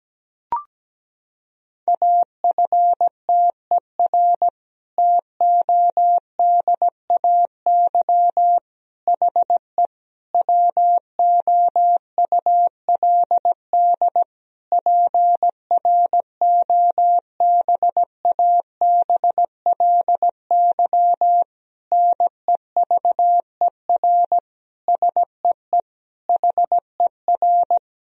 Morse code, heard as a keyed tone, about 17 wpm, 700 Hz. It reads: AFTER TODAY HE WOULD PROBABLY NEVER SEE HER
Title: Sentences from Top 1000 Words - Review Part 4 - 17wpm